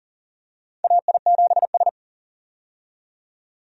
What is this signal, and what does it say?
Morse code, heard as a keyed tone, about 40 wpm, 700 Hz. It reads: AI7S
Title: USA Call Signs - 4 Characters - 40wpm